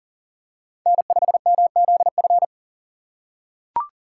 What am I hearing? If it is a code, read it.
Morse code, heard as a keyed tone, about 40 wpm, 700 Hz. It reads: N5MZF
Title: USA Call Signs - 5 Characters - 2x Character Spacing - 40wpm